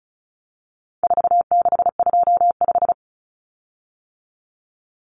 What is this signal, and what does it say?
Morse code, heard as a keyed tone, about 34 wpm, 700 Hz. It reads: 4625
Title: Common Cut Numbers - 4-Digit 35wpm